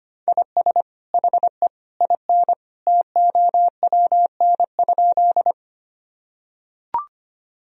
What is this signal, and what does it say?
Morse code, heard as a keyed tone, about 25 wpm, 700 Hz. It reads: IS HE IN TOWN?
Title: Sentences from Top 1000 Words - Review Part 2 - 25wpm